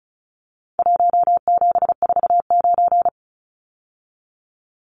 Morse code, heard as a keyed tone, about 34 wpm, 700 Hz. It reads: 1749
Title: Numbers - Contest Style - 4-Digit 35wpm